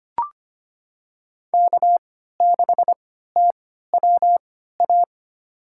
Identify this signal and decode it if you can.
Morse code, heard as a keyed tone, about 25 wpm, spaced as for 14 wpm, 700 Hz. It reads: K6TWA
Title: USA Call Signs - 5 Characters - 3x Character Spacing - 25wpm